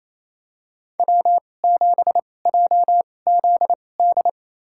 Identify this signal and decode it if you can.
Morse code, heard as a keyed tone, about 28 wpm, spaced as for 20 wpm, 700 Hz. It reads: W7JZD
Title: USA Call Signs - 5 Characters - 2x Character Spacing - 28wpm